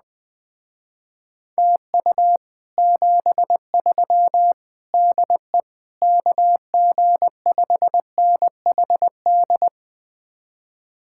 Morse code, heard as a keyed tone, about 20 wpm, 700 Hz. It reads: TU 73 DE KG5NHD
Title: Speed-Racing Common QSO Phrases - 17wpm